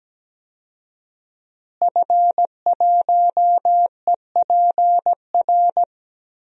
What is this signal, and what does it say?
Morse code, heard as a keyed tone, about 17 wpm, 700 Hz. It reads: F1EPR
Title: DXCC-100 Call Signs - 5 Characters - Repeated 3x - 17wpm